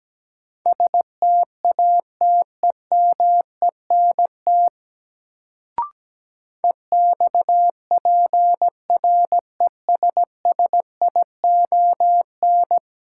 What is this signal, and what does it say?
Morse code, heard as a keyed tone, about 17 wpm, 700 Hz. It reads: STATEMENT EXPRESSION
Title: Sentences from Top 1200 Words - New Word Drill - 17wpm